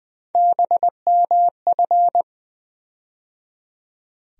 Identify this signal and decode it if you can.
Morse code, heard as a keyed tone, about 20 wpm, 700 Hz. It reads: BMF
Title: USA Call Sign Postfixes - 3 Letters 20wpm